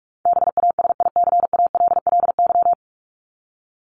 Morse code, heard as a keyed tone, about 44 wpm, 700 Hz. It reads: BASICALLY
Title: Top 1251-1500 Words 45wpm